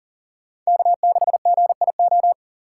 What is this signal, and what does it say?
Morse code, heard as a keyed tone, about 39 wpm, 700 Hz. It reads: K6GIO
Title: USA Call Signs - 5 Characters - 6x Character Spacing - 40wpm